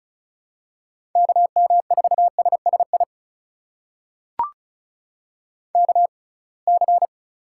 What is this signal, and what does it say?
Morse code, heard as a keyed tone, about 34 wpm, spaced as for 26 wpm, 700 Hz. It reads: KM4SSI KC
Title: USA Call Signs - 6 Characters - 6x Character Spacing - 35wpm